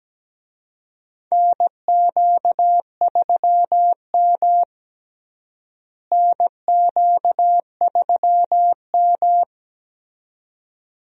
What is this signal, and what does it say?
Morse code, heard as a keyed tone, about 17 wpm, 700 Hz. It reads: NQ3M NQ3M
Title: USA Call Signs - 4 Characters - Repeated 2x - 17wpm